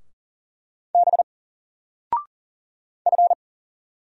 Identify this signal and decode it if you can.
Morse code, heard as a keyed tone, about 39 wpm, 700 Hz. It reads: B F
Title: Single Letter-Number 40wpm